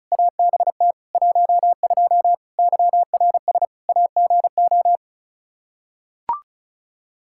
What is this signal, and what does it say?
Morse code, heard as a keyed tone, about 36 wpm, 700 Hz. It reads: ABT 12 YRS AGO
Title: Practice QSO - Exchange 3 - More Info - 35wpm